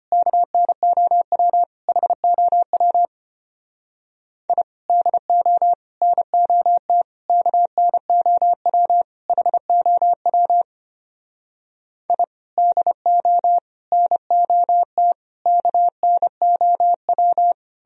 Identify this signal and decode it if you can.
Morse code, heard as a keyed tone, about 33 wpm, 700 Hz. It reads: KNOW HOW I DO NOT KNOW HOW I DO NOT KNOW